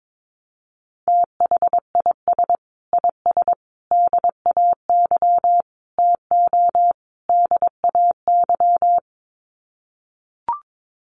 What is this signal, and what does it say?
Morse code, heard as a keyed tone, about 22 wpm, 700 Hz. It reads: THIS IS DAY TO DAY
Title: Sets of 5 Words Encore 22wpm